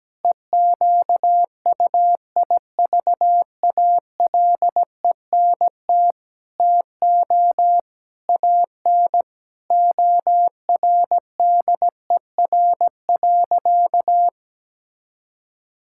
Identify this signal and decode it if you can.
Morse code, heard as a keyed tone, about 17 wpm, 700 Hz. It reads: EQUIVALENT TO AN ORDER.
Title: Sentences from Top 1600 Words 17wpm